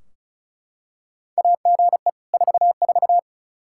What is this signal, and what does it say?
Morse code, heard as a keyed tone, about 36 wpm, 700 Hz. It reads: AGE 44
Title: Practice QSO - Exchange 3 - More Info - 35wpm